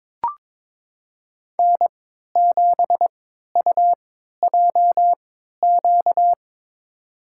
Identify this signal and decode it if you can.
Morse code, heard as a keyed tone, about 22 wpm, spaced as for 13 wpm, 700 Hz. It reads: N7UJQ
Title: USA Call Signs - 5 Characters - 3x Character Spacing - 22wpm